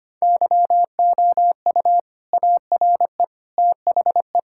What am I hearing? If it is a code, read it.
Morse code, heard as a keyed tone, about 25 wpm, 700 Hz. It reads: YOU ARE THE
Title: Sentences from Top 1000 Words - Review Part 5 - 25wpm